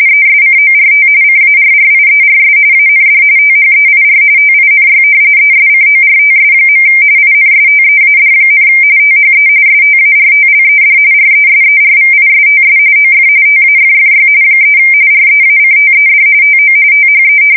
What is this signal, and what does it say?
RTTY, about 45.45 baud. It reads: ESCORT CARRIERS AND THEIR TINY ESCORTS. INSTEAD, THE STARTLED RESERVISTS OF TAFFY 3 REACTED TO THE
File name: fsk170 / battleshipnj